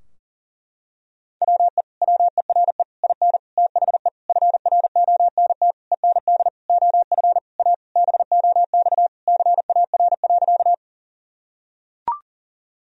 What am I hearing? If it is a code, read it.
Morse code, heard as a keyed tone, about 40 wpm, 700 Hz. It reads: WE WERE IN THE FRONT END OF A BOX CAR.
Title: Sentences from Top 1000 Words - Review Part 2 - 40wpm